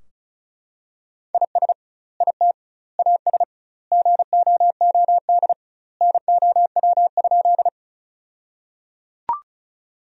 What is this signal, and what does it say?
Morse code, heard as a keyed tone, about 34 wpm, 700 Hz. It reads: IS IT AS GOOD NOW?
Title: 2x Word-Spacing - Sentences from Top 200 Words - 35wpm